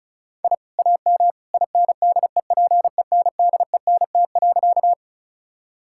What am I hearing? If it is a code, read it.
Morse code, heard as a keyed tone, about 34 wpm, 700 Hz. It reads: I AM INDEPENDENT.
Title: Sentences from Top 1000 Words - Review Part 6 - 35wpm